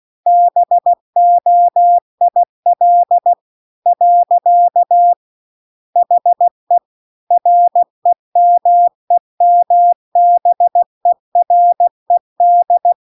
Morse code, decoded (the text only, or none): BOIL . HE REMEMBERED